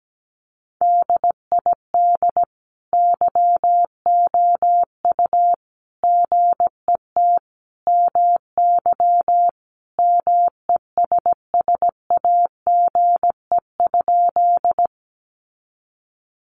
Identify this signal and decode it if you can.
Morse code, heard as a keyed tone, about 17 wpm, 700 Hz. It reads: DID YOU GET MY MESSAGE?
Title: Sentences from Top 1100 Words 17wpm